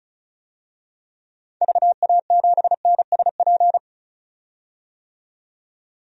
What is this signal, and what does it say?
Morse code, heard as a keyed tone, about 34 wpm, 700 Hz. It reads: VA7NSP
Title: DXCC-100 Call Signs - 6 Characters - 35wpm